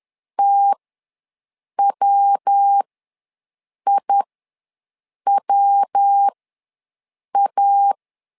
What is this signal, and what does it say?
Morse code, heard as a keyed tone, about 10 wpm, spaced as for 6 wpm, 800 Hz. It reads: TWIWA
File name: LEZIONI / 3 MWRO / 3 MW ws 10wpm 2213